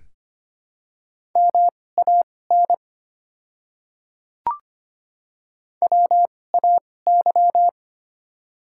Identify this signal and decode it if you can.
Morse code, heard as a keyed tone, about 25 wpm, spaced as for 18 wpm, 700 Hz. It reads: MAN WAY
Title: 2x Character Spacing - Top 300 Words - 25wpm